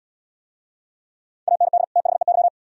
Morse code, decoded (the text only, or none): AND THEY